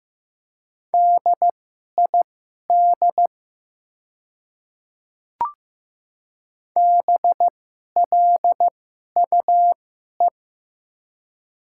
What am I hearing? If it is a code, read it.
Morse code, heard as a keyed tone, about 15 wpm, spaced as for 11 wpm, 700 Hz. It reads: DID BLUE
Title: 2x Character Spacing - Top 500 Words - 15wpm